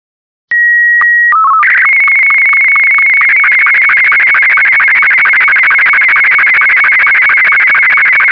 SSTV (slow-scan television), Martin 2.